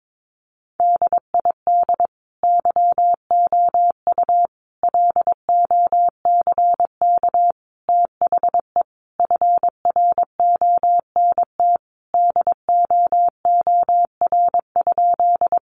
Morse code, heard as a keyed tone, about 22 wpm, 700 Hz. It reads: DID YOU LOCK THE FRONT DOOR?
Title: Sentences from Top 1300 Words 22wpm